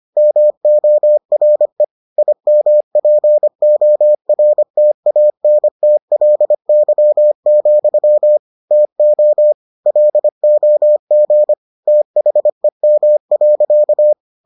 Morse code, heard as a keyed tone, about 25 wpm, 600 Hz. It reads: MORE IMPORTANTLY, TO LOG THEM.